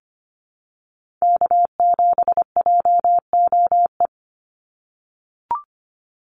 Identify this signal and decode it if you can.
Morse code, heard as a keyed tone, about 25 wpm, 700 Hz. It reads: K7JOE
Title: Call Signs - 25wpm